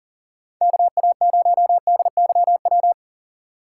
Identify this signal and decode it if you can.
Morse code, heard as a keyed tone, about 52 wpm, 700 Hz. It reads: KA0DYW